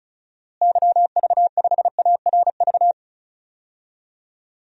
Morse code, heard as a keyed tone, about 36 wpm, 700 Hz. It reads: YV5ARV
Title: DXCC-100 Call Signs - 6 Characters - Repeated 3x - 35wpm